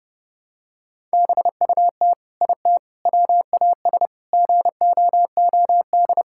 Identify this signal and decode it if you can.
Morse code, heard as a keyed tone, about 30 wpm, 700 Hz. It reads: BUT IT WAS GOOD
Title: Sentences from Top 1000 Words - Review Part 6 - 30wpm